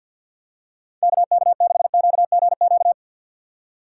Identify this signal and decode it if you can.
Morse code, heard as a keyed tone, about 47 wpm, 700 Hz. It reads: KK6QGQ
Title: USA Call Signs - 6 Characters - 4x Character Spacing - 50wpm